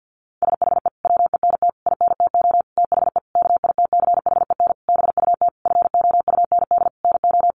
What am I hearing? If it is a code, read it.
Morse code, heard as a keyed tone, about 48 wpm, 700 Hz. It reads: SHE WENT INTO THE KITCHEN BUT FOUND NO
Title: Sentences from Top 1400 Words 50wpm